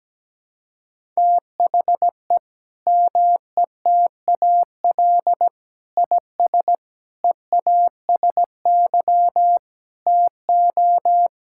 Morse code, heard as a keyed tone, about 17 wpm, 700 Hz. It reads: THE METAL IS EASY TO
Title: Sentences from Top 1000 Words - Review Part 6 - 17wpm